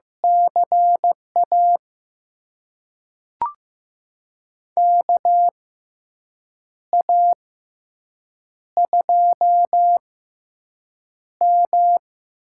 Morse code, heard as a keyed tone, about 15 wpm, spaced as for 6 wpm, 700 Hz. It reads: CA KA2M
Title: USA Call Signs - 6 Characters - 6x Character Spacing - 15wpm